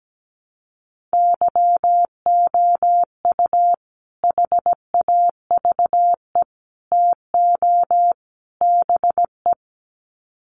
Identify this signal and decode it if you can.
Morse code, heard as a keyed tone, about 17 wpm, 700 Hz. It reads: YOU HAVE TO BE
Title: Sets of 4 Words 17wpm